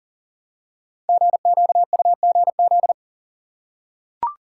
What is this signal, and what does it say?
Morse code, heard as a keyed tone, about 41 wpm, 700 Hz. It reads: GQUGZ
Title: Code Groups - 5 Letters - 40wpm